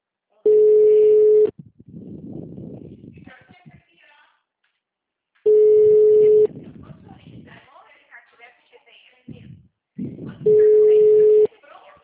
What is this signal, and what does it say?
Ringback tone (Europe, 425 Hz), with speech.